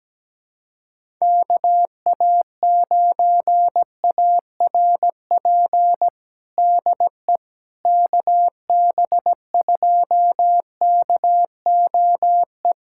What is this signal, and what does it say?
Morse code, heard as a keyed tone, about 17 wpm, 700 Hz. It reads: KA9ARP DE KB2KOE
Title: USA Callsign DE Callsign - 6 Characters - 17wpm